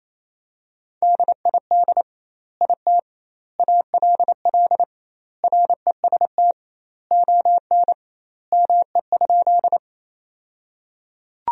Morse code, heard as a keyed tone, about 28 wpm, 700 Hz. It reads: DID IT ALL REST ON ME?
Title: 2x Word-Spacing - Sentences from Top 500 Words - 28wpm